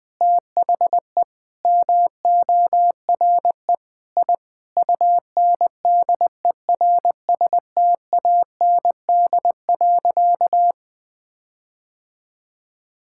Morse code, heard as a keyed tone, about 20 wpm, 700 Hz. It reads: THE MORE I UNDERSTAND.
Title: Sentences with 8-Word Phrases - 20wpm